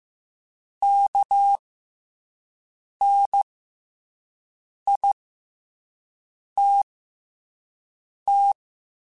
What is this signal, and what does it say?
Morse code, heard as a keyed tone, about 15 wpm, spaced as for 5 wpm, 800 Hz. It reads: KNITT
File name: Special INGs-mp3s / Special INGs-mp3s / SlowToFaster ing @10